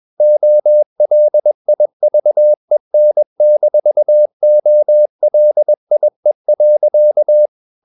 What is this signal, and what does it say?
Morse code, heard as a keyed tone, about 21 wpm, 600 Hz. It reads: OLIVEN-OLIE.
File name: Ferie QRS 21wpm